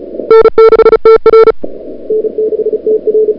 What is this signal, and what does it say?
Morse code, heard as a keyed tone, about 38 wpm, 440 Hz. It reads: N6TR N6TR
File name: n6tr / Fab2007CWSprintHour1